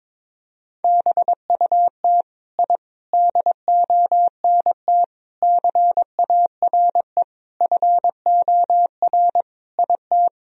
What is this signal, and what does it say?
Morse code, heard as a keyed tone, about 22 wpm, 700 Hz. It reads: BUT I DONT CARE FOR IT
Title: Sentences from Top 1000 Words - Review Part 1 - 22wpm